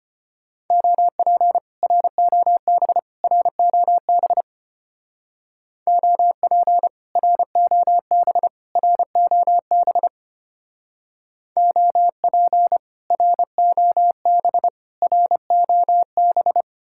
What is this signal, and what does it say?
Morse code, heard as a keyed tone, about 31 wpm, 700 Hz. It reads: OP ROB ROB OP ROB ROB OP ROB ROB
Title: Speed-Racing Common QSO Phrases - 25wpm